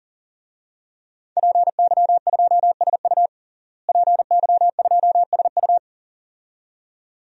Morse code, heard as a keyed tone, about 40 wpm, 700 Hz. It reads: PY2SU PY2SU
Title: DXCC-100 Call Signs - 5 Characters - Repeated 2x - 40wpm